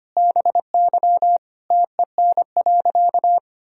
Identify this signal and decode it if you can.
Morse code, heard as a keyed tone, about 25 wpm, 700 Hz. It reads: BY TEN.
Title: Sentences from Top 1000 Words - Review Part 2 - 25wpm